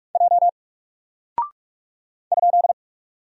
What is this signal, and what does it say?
Morse code, heard as a keyed tone, about 44 wpm, 700 Hz. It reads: J ?
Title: Single Character (Letter:Number:Punctuation:Prosign) 45wpm